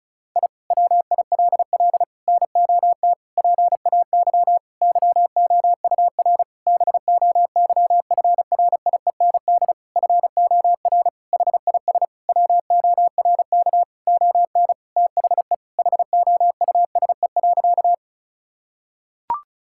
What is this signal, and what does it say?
Morse code, heard as a keyed tone, about 36 wpm, 700 Hz. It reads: I WILL NOT PAY YOUR BOYFRIEND FOR HIS WORK ON THE HOUSE.
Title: Sentences from Top 1000 Words - Review Part 5 - 35wpm